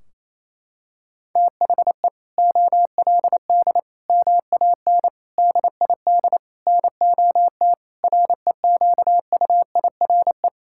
Morse code, heard as a keyed tone, about 28 wpm, 700 Hz. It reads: THE OLD MAN DID NOT REQUIRE
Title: Sentences from Top 1600 Words 28wpm